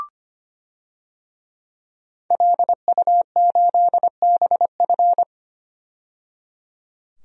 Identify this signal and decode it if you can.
Morse code, heard as a keyed tone, about 25 wpm, 700 Hz. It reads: LU8BF
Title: DXCC-100 Call Signs - 5 Characters - 25wpm